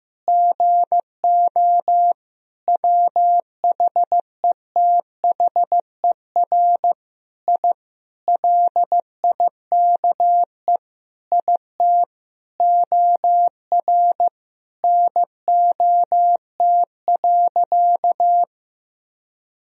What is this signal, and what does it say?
Morse code, heard as a keyed tone, about 15 wpm, 700 Hz. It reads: GO WHETHER I LIKE IT OR NOT.
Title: Sentences from Top 1000 Words - Review Part 3 - 15wpm